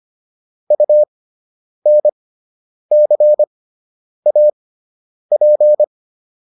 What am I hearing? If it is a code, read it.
Morse code, heard as a keyed tone, about 25 wpm, spaced as for 9 wpm, 600 Hz. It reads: UNCAP